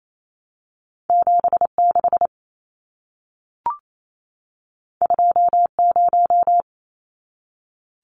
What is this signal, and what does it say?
Morse code, heard as a keyed tone, about 28 wpm, 700 Hz. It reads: 76 20
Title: Numbers - 2-Digit 28wpm